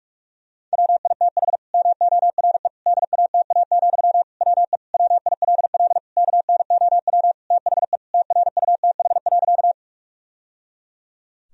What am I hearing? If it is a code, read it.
Morse code, heard as a keyed tone, about 41 wpm, 700 Hz. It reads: WITH MORE DATA, WE WILL KNOW THE TRUTH.
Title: Sentences from Top 1000 Words - Review Part 3 - 45wpm